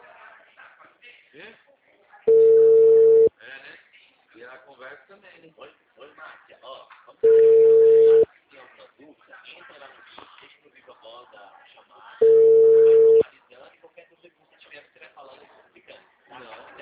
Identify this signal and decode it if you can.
Ringback tone (China, 450 Hz), with speech.
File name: out- -2011